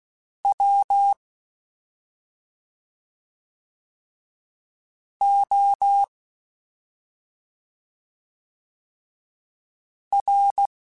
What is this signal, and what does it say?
Morse code, heard as a keyed tone, about 16 wpm, spaced as for 2 wpm, 800 Hz. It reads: WOR
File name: Special INGs-mp3s / Special INGs-mp3s / Slow ing @10